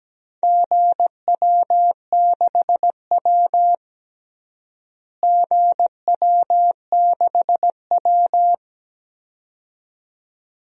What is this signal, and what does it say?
Morse code, heard as a keyed tone, about 17 wpm, 700 Hz. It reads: GW6W GW6W
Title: DXCC-100 Call Signs - 4 Characters - Repeated 2x - 17wpm